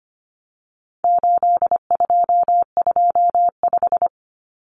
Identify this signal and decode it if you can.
Morse code, heard as a keyed tone, about 25 wpm, 700 Hz. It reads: 8225
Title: Common Cut Numbers - 4-Digit 25wpm